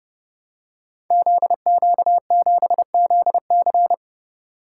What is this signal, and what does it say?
Morse code, heard as a keyed tone, about 30 wpm, 700 Hz. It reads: ZQ7ZC
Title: Code Groups - 5 Letter:Numbers - 30wpm